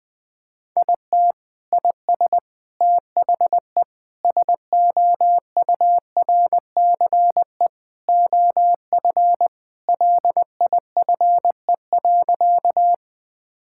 Morse code, heard as a keyed tone, about 19 wpm, 700 Hz. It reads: IT IS THE SOURCE OF LIFE.
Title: Sentences from Top 1000 Words - Review Part 5 - 20wpm